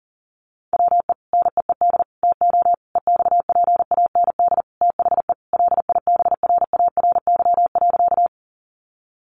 Morse code, heard as a keyed tone, about 39 wpm, 700 Hz. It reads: WE NEED TO EXPAND THE LIBRARY.
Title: Sentences from Top 1400 Words 40wpm